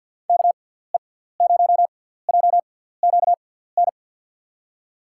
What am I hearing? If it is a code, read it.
Morse code, heard as a keyed tone, about 46 wpm, spaced as for 17 wpm, 700 Hz. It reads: KE0JQN
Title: USA Call Signs - 6x Character Spacing - 50wpm